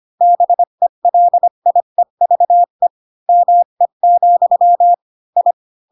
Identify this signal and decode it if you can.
Morse code, heard as a keyed tone, about 25 wpm, 700 Hz. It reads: BELIEVE ME, I
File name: Pinocchio / Pinocchio ch7 20 25 ik1hge